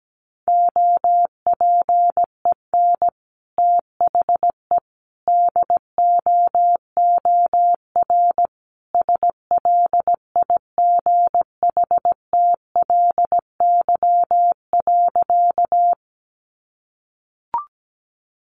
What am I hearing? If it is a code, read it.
Morse code, heard as a keyed tone, about 17 wpm, 700 Hz. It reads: OPEN THE DOOR SLIGHTLY.
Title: Sentences from Top 1300 Words 17wpm